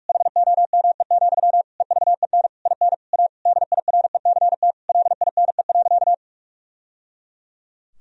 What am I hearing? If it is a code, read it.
Morse code, heard as a keyed tone, about 46 wpm, 700 Hz. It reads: HOME, EVEN IN A DIRECT LINE.
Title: Sentences from Top 1000 Words - Review Part 2 - 45wpm